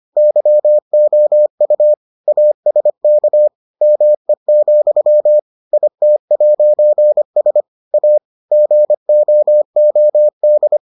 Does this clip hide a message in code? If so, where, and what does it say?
Yes: Morse code from 0.2 to 10.8 s, heard as a keyed tone, about 25 wpm, 600 Hz. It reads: YOU ASK ME, IT'S A GOOD